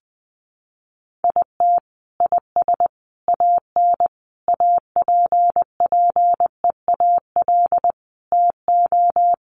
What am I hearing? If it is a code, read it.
Morse code, heard as a keyed tone, about 20 wpm, 700 Hz. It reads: IT IS AN APPEAL TO
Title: Sentences from Top 1300 Words 20wpm